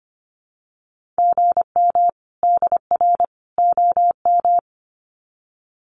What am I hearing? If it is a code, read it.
Morse code, heard as a keyed tone, about 25 wpm, 700 Hz. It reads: GM DR OM
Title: Common QSO Phrases 25wpm